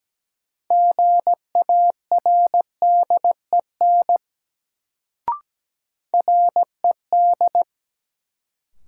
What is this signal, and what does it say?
Morse code, heard as a keyed tone, about 17 wpm, 700 Hz. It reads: GARDEN RED
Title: Sentences from Top 700 Words - New Word Drill - 17wpm